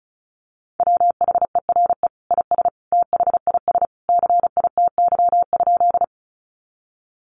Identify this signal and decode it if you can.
Morse code, heard as a keyed tone, about 34 wpm, 700 Hz. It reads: WHERE IS THIS CITY?
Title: Sentences from Top 300 Words 35wpm